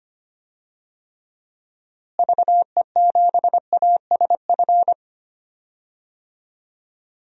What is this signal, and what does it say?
Morse code, heard as a keyed tone, about 25 wpm, 700 Hz. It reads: VE7ASF
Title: DXCC-100 Call Signs - 6 Characters - Repeated 3x - 25wpm